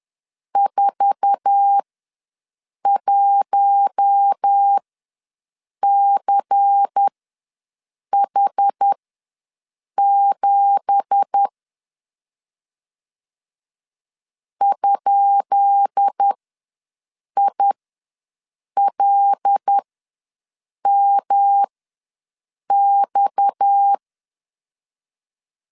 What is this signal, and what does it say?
Morse code, heard as a keyed tone, about 10 wpm, spaced as for 6 wpm, 800 Hz. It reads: 41CH7 ?ILMX
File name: LEZIONI / 10 SIGNS / 10B INT ws 10wpm 4134